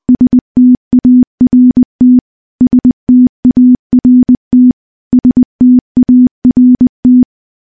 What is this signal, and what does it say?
Morse code, heard as a keyed tone, about 20 wpm, 250 Hz. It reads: START START START
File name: morse-training-20WPM